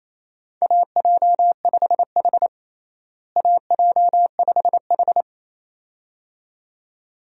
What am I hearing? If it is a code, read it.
Morse code, heard as a keyed tone, about 28 wpm, 700 Hz. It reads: AJ5H AJ5H
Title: USA Call Signs - 4 Characters - Repeated 2x - 28wpm